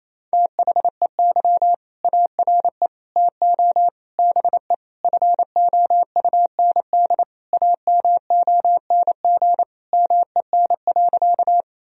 Morse code, heard as a keyed tone, about 28 wpm, 700 Hz. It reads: THEY ARE TO BE FOUND AMONG MEN.
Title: Sentences from Top 1000 Words - Review Part 3 - 28wpm